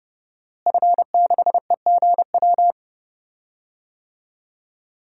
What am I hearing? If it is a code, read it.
Morse code, heard as a keyed tone, about 30 wpm, 700 Hz. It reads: F6EGW
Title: DXCC-100 Call Signs - 5 Characters - Repeated 3x - 30wpm